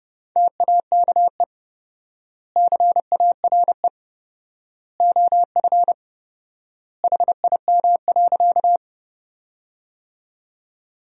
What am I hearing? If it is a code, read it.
Morse code, heard as a keyed tone, about 30 wpm, 700 Hz. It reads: TAKE CARE OF HIM.
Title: 4x Word-Spacing - Sentences from Top 300 Words - 30wpm